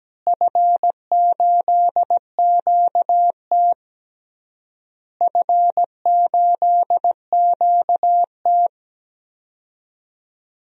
Morse code, heard as a keyed tone, about 17 wpm, 700 Hz. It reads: F8QT F8QT
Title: DXCC-100 Call Signs - 4 Characters - Repeated 3x - 17wpm